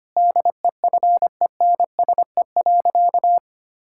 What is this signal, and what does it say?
Morse code, heard as a keyed tone, about 25 wpm, 700 Hz. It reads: DEFENSE.
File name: practice / 25wpm / Difficult Sentences with Hints - 25wpm